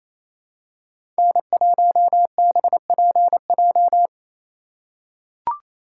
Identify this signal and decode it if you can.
Morse code, heard as a keyed tone, about 28 wpm, 700 Hz. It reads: N1BPJ
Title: USA Call Signs - 4x Character Spacing - 28wpm